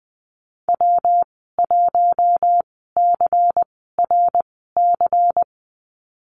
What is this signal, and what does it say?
Morse code, heard as a keyed tone, about 20 wpm, spaced as for 14 wpm, 700 Hz. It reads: W1CRC
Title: 122 Lesson 20g - Number 3 - Call Signs 20wpm